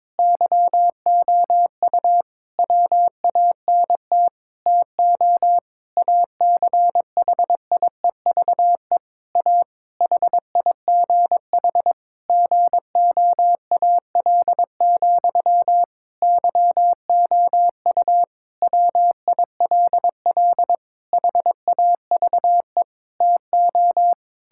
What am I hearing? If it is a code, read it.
Morse code, heard as a keyed tone, about 22 wpm, 700 Hz. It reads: YOU WANT TO ACHIEVE A HIGH GOAL, YOU WILL HAVE TO
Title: Sentences from Top 1000 Words - Review Part 6 - 22wpm